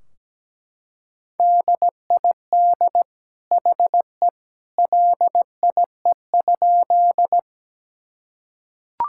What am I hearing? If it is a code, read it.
Morse code, heard as a keyed tone, about 17 wpm, 700 Hz. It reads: DID HE LIE?
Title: Sentences from Top 1600 Words 17wpm